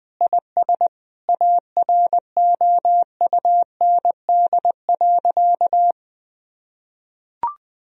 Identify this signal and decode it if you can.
Morse code, heard as a keyed tone, about 21 wpm, 700 Hz. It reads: IS AROUND.
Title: Sentences from Top 1000 Words - Review Part 4 - 20wpm